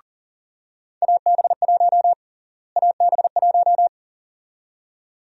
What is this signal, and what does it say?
Morse code, heard as a keyed tone, about 40 wpm, 700 Hz. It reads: AB1 AB1
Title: USA Call Sign Prefixes - Repeated 2x - 40wpm